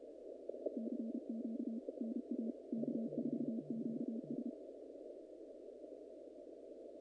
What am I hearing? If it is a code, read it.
Morse code, heard as a keyed tone, about 30 wpm, 210 Hz. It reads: CQ NA K4QS
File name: contests / naqp-cw-k5tr-op-k5pi- -2100